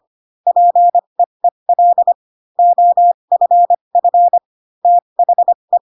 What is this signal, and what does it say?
Morse code, heard as a keyed tone, about 25 wpm, spaced as for 22 wpm, 700 Hz. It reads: PEEL OFF THE